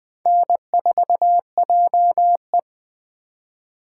Morse code, heard as a keyed tone, about 20 wpm, 700 Hz. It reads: N4JE